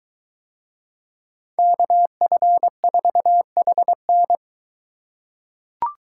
Morse code, heard as a keyed tone, about 22 wpm, 700 Hz. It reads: KF4HN